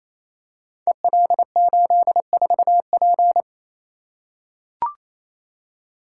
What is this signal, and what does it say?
Morse code, heard as a keyed tone, about 28 wpm, 700 Hz. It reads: EL84P
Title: Code Groups - 5 Letter:Numbers - 28wpm